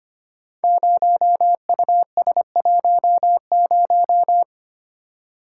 Morse code, heard as a keyed tone, about 25 wpm, 700 Hz. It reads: 0US10
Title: Code Groups - 5 Letter:Numbers - 25wpm